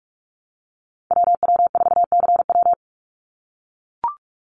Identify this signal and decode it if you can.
Morse code, heard as a keyed tone, about 44 wpm, 700 Hz. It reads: WW4CW
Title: USA Call Signs 45wpm